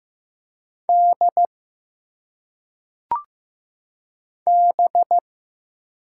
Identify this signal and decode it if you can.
Morse code, heard as a keyed tone, about 15 wpm, 700 Hz. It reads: D B
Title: Single Letters - Phonetic Pronunciation 15wpm